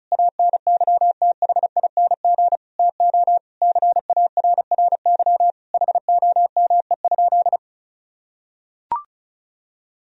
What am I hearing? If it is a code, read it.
Morse code, heard as a keyed tone, about 36 wpm, 700 Hz. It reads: ANYTHING TO CARRY HOME?
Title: Sentences from Top 1000 Words - Review Part 1 - 35wpm